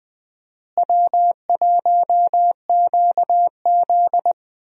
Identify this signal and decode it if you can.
Morse code, heard as a keyed tone, about 20 wpm, 700 Hz. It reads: W1QZ
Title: USA Call Signs - 4 Characters - Repeated 2x - 20wpm